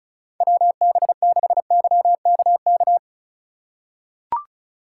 Morse code, heard as a keyed tone, about 36 wpm, 700 Hz. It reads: WB6YKK
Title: USA Call Signs - 6 Characters - 3x Character Spacing - 35wpm